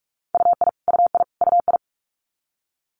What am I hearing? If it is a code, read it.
Morse code, heard as a keyed tone, about 46 wpm, 700 Hz. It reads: UI UI UI
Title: Multi-letter Phonemes Repeated 3x 45wpm